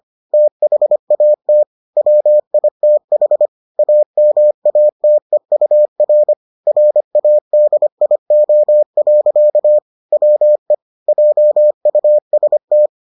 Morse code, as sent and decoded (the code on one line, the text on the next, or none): - .... .- - / .-- .. - .... / .- -- .- - . ..- .-. / .-. .- -.. .. --- .-.-.- / .-- . / .--- ..- ... -
THAT WITH AMATEUR RADIO. WE JUST